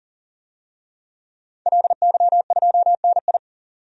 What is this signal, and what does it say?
Morse code, heard as a keyed tone, about 40 wpm, 700 Hz. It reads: LY2NI
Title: DXCC-100 Call Signs - 5 Characters - 40wpm